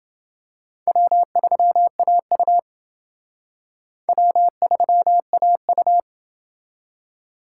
Morse code, heard as a keyed tone, about 28 wpm, 700 Hz. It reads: W3AU W3AU